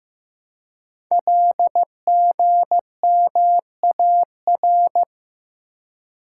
Morse code, heard as a keyed tone, about 15 wpm, 700 Hz. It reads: LGMAR
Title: Code Groups - 5 Letters - 15wpm